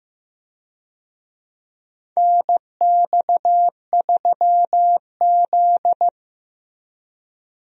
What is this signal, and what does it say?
Morse code, heard as a keyed tone, about 15 wpm, 700 Hz. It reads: NX3Z
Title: USA Call Signs - 4 Characters - Repeated 3x - 15wpm